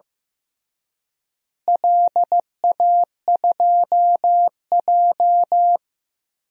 Morse code, heard as a keyed tone, about 15 wpm, 700 Hz. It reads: LA2J
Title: DXCC-100 Call Signs - 4 Characters - Repeated 3x - 15wpm